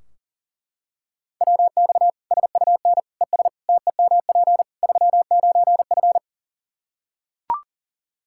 Morse code, heard as a keyed tone, about 40 wpm, 700 Hz. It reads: WX SUN ES TEMP 39F
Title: Practice QSO - Exchange 2 - Info - 40wpm